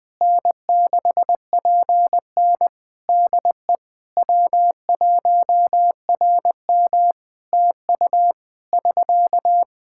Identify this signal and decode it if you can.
Morse code, heard as a keyed tone, about 20 wpm, 700 Hz. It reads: N6PN DE W1RM TU <SK>
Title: Practice QSO - Exchange 4 - Ending - 20wpm